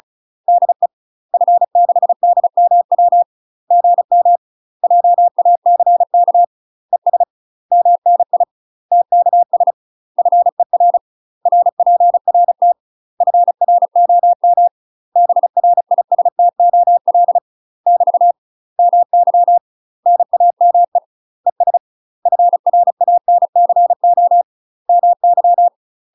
Morse code, heard as a keyed tone, about 34 wpm, 700 Hz. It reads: DE F6DMW GM JACK ES MNI TKS FER RPRT FROM BRISTOL <BT> MY NAME ES FRANCO MY
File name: UK MASTER QSOs TRAINING / 35WPM / UK MASTER QSOs 0009